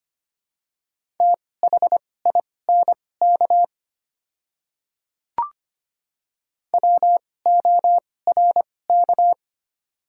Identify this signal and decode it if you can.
Morse code, heard as a keyed tone, about 25 wpm, spaced as for 18 wpm, 700 Hz. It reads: THINK WORK
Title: 2x Character Spacing - Top 200 Words - 25wpm